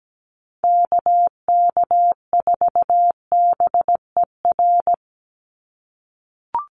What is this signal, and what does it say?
Morse code, heard as a keyed tone, about 17 wpm, 700 Hz. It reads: KK4BER
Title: USA Call Signs 17wpm